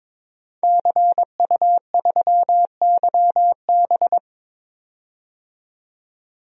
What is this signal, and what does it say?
Morse code, heard as a keyed tone, about 22 wpm, 700 Hz. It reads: CU3YB
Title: DXCC-100 Call Signs - 5 Characters - 22wpm